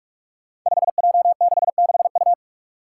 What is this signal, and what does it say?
Morse code, heard as a keyed tone, about 44 wpm, 700 Hz. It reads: 5J66U
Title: Five Challenging Letter-Number Combinations 45wpm